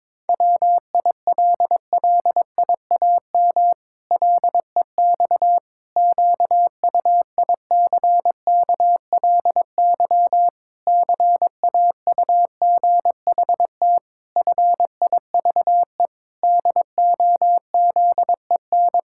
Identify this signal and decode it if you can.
Morse code, heard as a keyed tone, about 22 wpm, 700 Hz. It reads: WILLIAM LEX QUICKLY CAUGHT FIVE DOZEN
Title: Pangram Challenge v2 - 22wpm